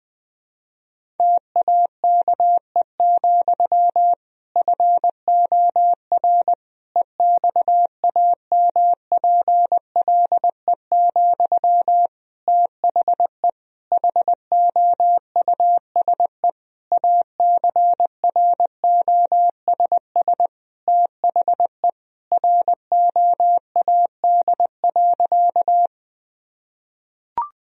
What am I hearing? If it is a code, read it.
Morse code, heard as a keyed tone, about 21 wpm, 700 Hz. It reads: TAKE, FOR EXAMPLE, THE HOUSE ACROSS THE ROAD.
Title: Sentences from Top 1000 Words - Review Part 1 - 20wpm